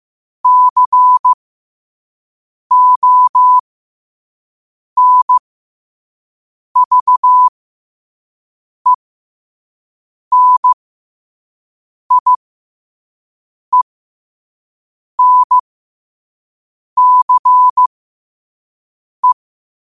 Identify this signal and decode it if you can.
Morse code, heard as a keyed tone, about 15 wpm, spaced as for 5 wpm, 1000 Hz. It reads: CONVENIENCE
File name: qotdfiles / 5wpm